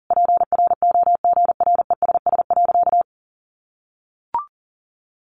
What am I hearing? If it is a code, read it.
Morse code, heard as a keyed tone, about 40 wpm, 700 Hz. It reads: PROGRESS.
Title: Sentences from Top 1400 Words 40wpm